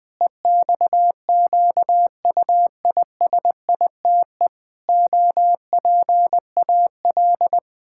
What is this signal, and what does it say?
Morse code, heard as a keyed tone, about 20 wpm, 700 Hz. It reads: EXQUISITE OPAL
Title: Pangram Challenge v2 - 20wpm